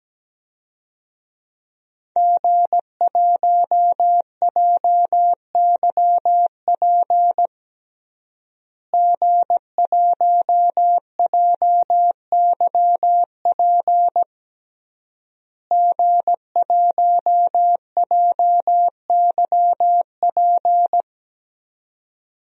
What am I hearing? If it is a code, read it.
Morse code, heard as a keyed tone, about 17 wpm, 700 Hz. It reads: G1JYP G1JYP G1JYP
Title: DXCC-100 Call Signs - 5 Characters - Repeated 3x - 17wpm